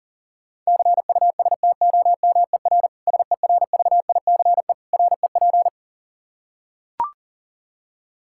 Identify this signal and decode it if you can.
Morse code, heard as a keyed tone, about 37 wpm, 700 Hz. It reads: CUSTOMER SERVICE REP
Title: Practice QSO - Exchange 3 - More Info - 40wpm